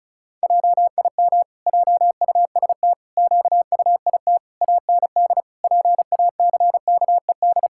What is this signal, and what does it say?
Morse code, heard as a keyed tone, about 34 wpm, 700 Hz. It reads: JIM JUST QUIT AND PACKED
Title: Pangram Challenge v2 - 35wpm